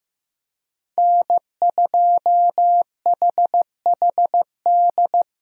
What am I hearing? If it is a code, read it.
Morse code, heard as a keyed tone, about 15 wpm, 700 Hz. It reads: N2HHD
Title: USA Call Signs - 5 Characters - Repeated 2x - 15wpm